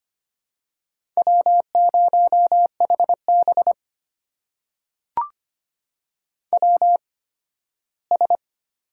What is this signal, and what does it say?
Morse code, heard as a keyed tone, about 25 wpm, spaced as for 15 wpm, 700 Hz. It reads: W0HB WS
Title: USA Call Signs - 8x Character Spacing - 25wpm